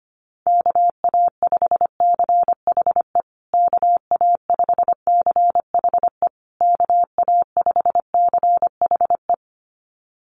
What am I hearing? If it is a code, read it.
Morse code, heard as a keyed tone, about 25 wpm, 700 Hz. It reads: KA5CHE KA5CHE KA5CHE
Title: USA Call Signs Repeated 3x 25wpm